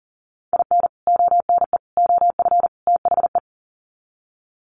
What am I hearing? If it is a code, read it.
Morse code, heard as a keyed tone, about 38 wpm, 700 Hz. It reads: IN ONE OF THE
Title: Sets of 4 Words 40wpm